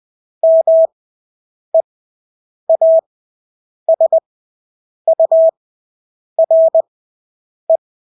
Morse code, heard as a keyed tone, about 20 wpm, spaced as for 8 wpm, 650 Hz. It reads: MEASURE
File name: Engelske ord 20/8 wpm